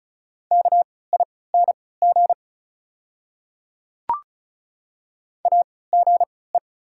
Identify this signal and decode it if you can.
Morse code, heard as a keyed tone, about 36 wpm, spaced as for 20 wpm, 690 Hz. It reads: KING AGE
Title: 3x Character Spacing - Top 500 Words - 35wpm